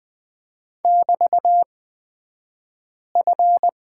Morse code, heard as a keyed tone, about 20 wpm, spaced as for 5 wpm, 700 Hz. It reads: <BT>F
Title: 223 Lesson 36d - BT - Two Characters 20wpm